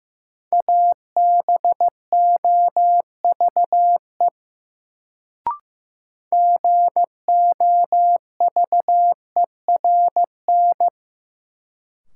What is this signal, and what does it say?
Morse code, heard as a keyed tone, about 16 wpm, 700 Hz. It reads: ABOVE GOVERN
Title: Sentences from Top 1000 Words - Review Part 2 - New Word Drill - 15wpm